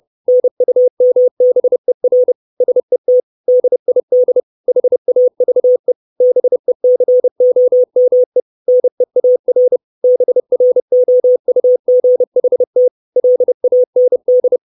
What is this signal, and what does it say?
Morse code, heard as a keyed tone, about 30 wpm, 500 Hz. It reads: NUMBER SET DID HAVE BECOME NEAR BROUGHT LAND
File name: 632 Words 30WPM